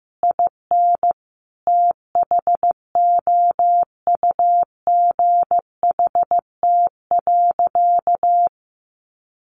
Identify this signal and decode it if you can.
Morse code, heard as a keyed tone, about 14 wpm, 700 Hz. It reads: IN THOUGHT.
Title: Sentences from Top 1400 Words 15wpm